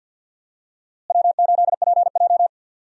Morse code, heard as a keyed tone, about 46 wpm, 700 Hz. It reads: W8PJ
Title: USA Call Signs - 4 Characters - Repeated 3x - 50wpm